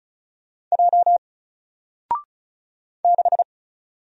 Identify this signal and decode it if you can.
Morse code, heard as a keyed tone, about 34 wpm, 700 Hz. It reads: J 6
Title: Single Letter-Number Alt 35wpm